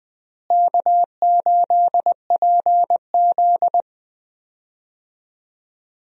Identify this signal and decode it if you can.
Morse code, heard as a keyed tone, about 20 wpm, 700 Hz. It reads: K8PZ
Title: USA Call Signs - 4 Characters - Repeated 3x - 20wpm